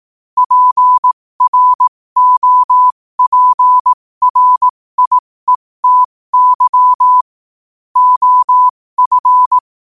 Morse code, heard as a keyed tone, about 18 wpm, spaced as for 15 wpm, 1000 Hz. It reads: PROPRIETY OF
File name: qotdfiles / 15wpm